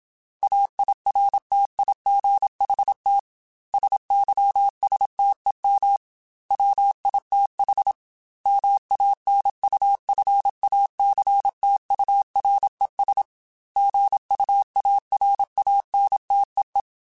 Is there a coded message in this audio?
Yes: Morse code from 0.4 to 16.8 s, heard as a keyed tone, about 27 wpm, 800 Hz. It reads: AIRTIGHT SYSTEM WITH MANUFACTURES GUARANTEE